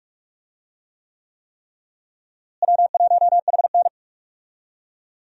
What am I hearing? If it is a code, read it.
Morse code, heard as a keyed tone, about 44 wpm, 700 Hz. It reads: W1HN
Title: USA Call Signs - 4 Characters - Repeated 3x - 45wpm